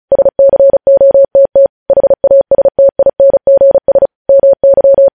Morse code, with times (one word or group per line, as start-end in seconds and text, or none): 0.1-1.7 SCOTT
1.9-4.1 HASTINGS
4.3-5.1 MY